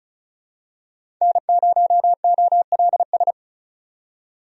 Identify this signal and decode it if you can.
Morse code, heard as a keyed tone, about 36 wpm, 700 Hz. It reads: N0OLS
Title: USA Call Signs - 5 Characters - 35wpm